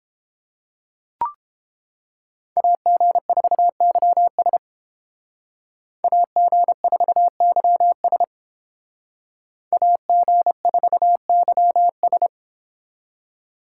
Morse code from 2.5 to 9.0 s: AG4YS AG4YS

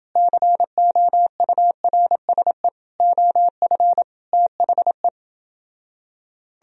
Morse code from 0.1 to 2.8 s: COURSE

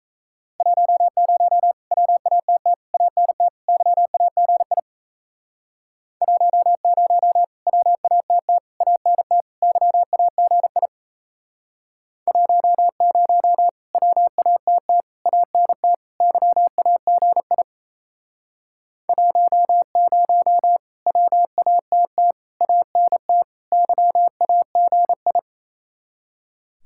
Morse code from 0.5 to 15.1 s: 10 WATT ANT YAGI 10 WATT ANT YAGI 10 WATT

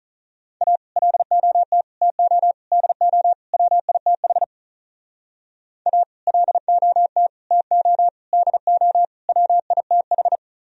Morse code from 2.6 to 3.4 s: DO

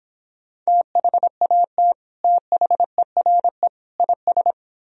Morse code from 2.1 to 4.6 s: THERE IS

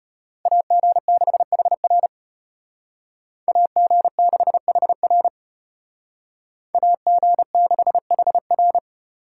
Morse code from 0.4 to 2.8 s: AG6HR

